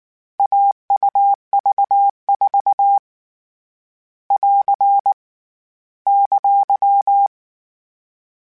Morse code, heard as a keyed tone, about 19 wpm, 800 Hz. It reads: AUV4 <AR> !
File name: Difficult Combos / Difficult Combos-mp3s / AUV AUV4 @18